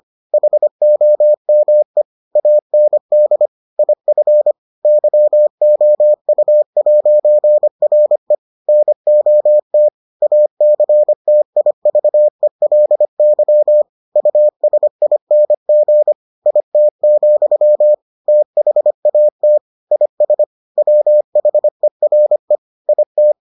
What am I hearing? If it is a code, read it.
Morse code, heard as a keyed tone, about 25 wpm, 600 Hz. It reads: HOME AND IF YOU'RE NOT ACTIVELY USING IT, THAT IS WHERE IT